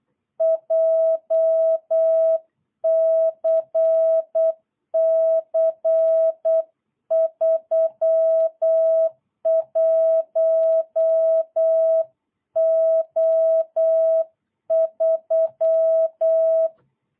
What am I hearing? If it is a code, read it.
Morse code, heard as a keyed tone, about 8 wpm, 650 Hz. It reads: JCC 31O3
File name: jh8cbh / 2022cw